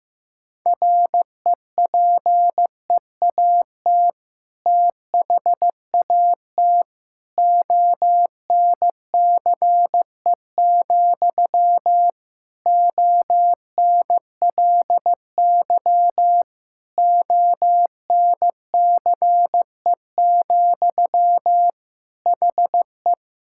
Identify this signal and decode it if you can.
Morse code, heard as a keyed tone, about 15 wpm, 700 Hz. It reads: REPEAT THAT ONCE, ONLY ONCE, HE
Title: Sentences from Top 1600 Words 15wpm